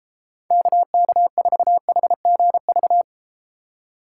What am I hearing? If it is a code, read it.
Morse code, heard as a keyed tone, about 33 wpm, 700 Hz. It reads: KK4HGV